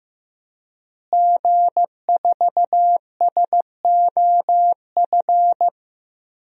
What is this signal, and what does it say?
Morse code, heard as a keyed tone, about 15 wpm, 700 Hz. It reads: G4SOF
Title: DXCC-100 Call Signs - 5 Characters - Repeated 2x - 15wpm